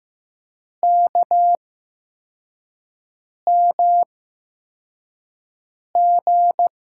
Morse code, heard as a keyed tone, about 15 wpm, spaced as for 4 wpm, 700 Hz. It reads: KMG
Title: USA Call Signs - 6 Characters - 8x Character Spacing - 15wpm